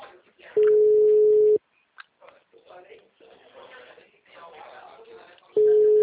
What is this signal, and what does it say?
Ringback tone (Europe, 425 Hz), with speech.